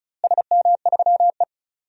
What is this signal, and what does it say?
Morse code, heard as a keyed tone, about 34 wpm, 700 Hz. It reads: SM3E
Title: DXCC-100 Call Signs - 4 Characters - Repeated 3x - 35wpm